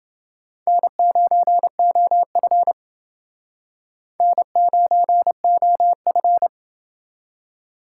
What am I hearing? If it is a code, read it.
Morse code, heard as a keyed tone, about 28 wpm, 700 Hz. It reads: N9OF N9OF